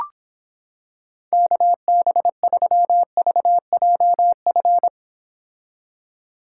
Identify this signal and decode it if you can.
Morse code, heard as a keyed tone, about 26 wpm, 700 Hz. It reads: KB3VJF